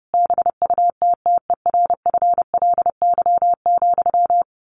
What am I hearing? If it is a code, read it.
Morse code, heard as a keyed tone, about 30 wpm, 700 Hz. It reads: BUTTERFLY,
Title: A Sound Of Thunder 30wpm